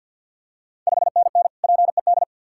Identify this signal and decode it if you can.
Morse code, heard as a keyed tone, about 51 wpm, 690 Hz. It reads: 5NN PED
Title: Ontario QSO Party Exchange 50wpm